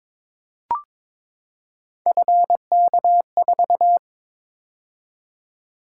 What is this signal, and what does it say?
Morse code, heard as a keyed tone, about 22 wpm, 700 Hz. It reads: FK4
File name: practice / 22wpm / Three Letter-Number Combinations - 22wpm